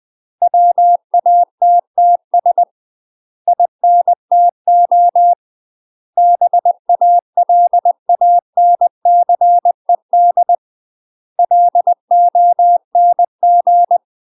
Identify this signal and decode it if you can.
Morse code, heard as a keyed tone, about 20 wpm, 700 Hz. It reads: WATTS INTO BALANCED LONG
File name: UK MASTER QSOs TRAINING/20WPM